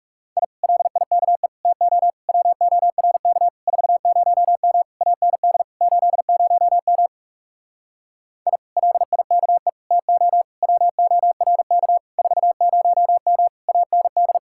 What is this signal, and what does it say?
Morse code, heard as a keyed tone, about 39 wpm, 700 Hz. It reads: I LIKE TO WORK 40M AND 80M I LIKE TO WORK 40M AND